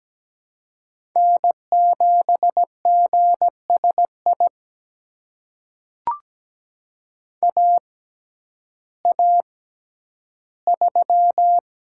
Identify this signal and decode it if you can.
Morse code, heard as a keyed tone, about 17 wpm, spaced as for 10 wpm, 700 Hz. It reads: N7GSI AA3
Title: USA Call Signs - 5 Characters - 6x Character Spacing - 17wpm